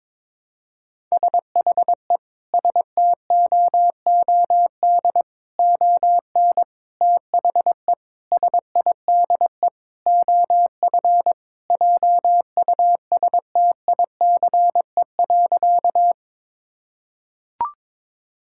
Morse code, heard as a keyed tone, about 22 wpm, 700 Hz. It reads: SHE STOOD ON THE SIDE OF JUSTICE.
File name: practice / 22wpm / Sentences with 4-Word Phrases - 22wpm